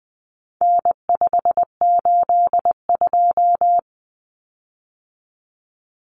Morse code, heard as a keyed tone, about 20 wpm, 700 Hz. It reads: N582
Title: Common Cut Numbers - 4-Digit 20wpm